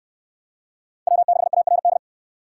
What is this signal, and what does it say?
Morse code, heard as a keyed tone, about 59 wpm, 700 Hz. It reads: W6ARD